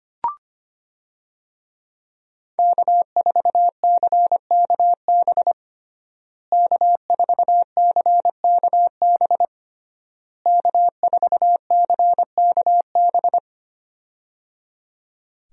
Morse code, heard as a keyed tone, about 25 wpm, 700 Hz. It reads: K4CKB K4CKB K4CKB
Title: USA Call Signs - 5 Characters - Repeated 3x - 25wpm